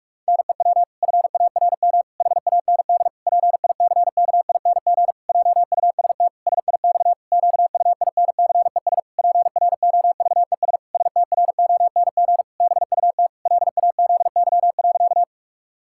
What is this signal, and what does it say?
Morse code, heard as a keyed tone, about 46 wpm, 700 Hz. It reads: NEW FARM HAND PICKING JUST SIX QUINCES PROVES STRONG BUT LAZY.
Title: Pangram Challenge v2 - 45wpm